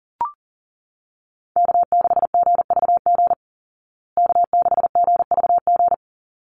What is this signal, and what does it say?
Morse code, heard as a keyed tone, about 40 wpm, 700 Hz. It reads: K6GVG K6GVG
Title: Call Signs Repeated 3x - 40wpm